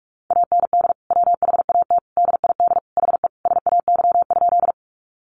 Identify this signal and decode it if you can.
Morse code, heard as a keyed tone, about 44 wpm, 700 Hz. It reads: AND WHAT DID HE SAY?
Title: Sentences from Top 200 Words 45wpm